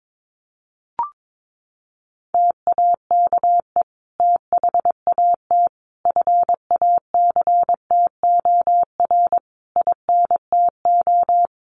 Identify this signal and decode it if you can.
Morse code, heard as a keyed tone, about 22 wpm, 700 Hz. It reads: TAKE THAT FACTOR INTO
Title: Sentences from Top 1300 Words 22wpm